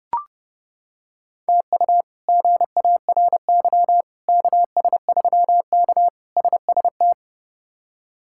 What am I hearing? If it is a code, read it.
Morse code, heard as a keyed tone, about 30 wpm, 700 Hz. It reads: TU GARY KS3K SST
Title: K1USN SST QSO Practice - 30wpm